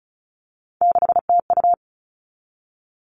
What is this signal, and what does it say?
Morse code, heard as a keyed tone, about 34 wpm, 700 Hz. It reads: 6TU
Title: All Cut Numbers - 3-Digit 35wpm